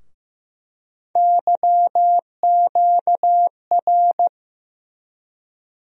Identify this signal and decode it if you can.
Morse code, heard as a keyed tone, about 15 wpm, 700 Hz. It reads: YQR
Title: USA Call Sign Postfixes - 3 Letters 15wpm